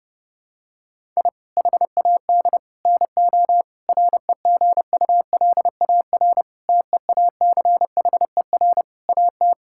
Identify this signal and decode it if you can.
Morse code, heard as a keyed tone, about 30 wpm, 700 Hz. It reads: I HAD NO REGULAR TEACHER AT
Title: Sentences from Top 1000 Words - Review Part 6 - 30wpm